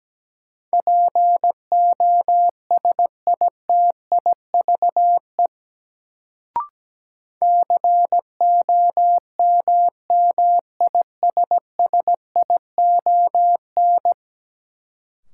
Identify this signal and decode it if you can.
Morse code, heard as a keyed tone, about 17 wpm, 700 Hz. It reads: POSITIVE COMMISSION
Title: Sentences from Top 1400 Words - New Word Drill Alt - 17wpm